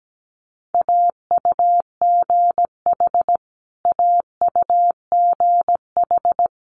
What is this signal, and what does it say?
Morse code, heard as a keyed tone, about 17 wpm, 700 Hz. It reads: AUGH AUGH
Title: Multi-letter Phonemes Repeated 3x 17wpm